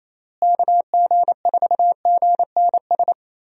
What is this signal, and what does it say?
Morse code, heard as a keyed tone, about 27 wpm, 700 Hz. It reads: KG4GNS
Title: USA Call Signs - 6 Characters - Repeated 2x - 28wpm